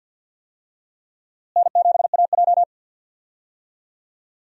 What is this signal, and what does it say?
Morse code, heard as a keyed tone, about 53 wpm, 690 Hz. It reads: N7AJ
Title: USA Call Signs - 4 Characters - Repeated 2x - 50wpm